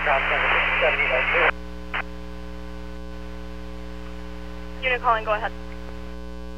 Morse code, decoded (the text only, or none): none